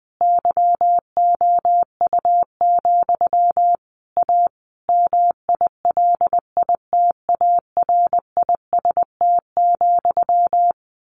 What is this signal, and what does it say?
Morse code, heard as a keyed tone, about 20 wpm, 700 Hz. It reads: YOU, A MILITARIST,
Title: A Sound Of Thunder 20wpm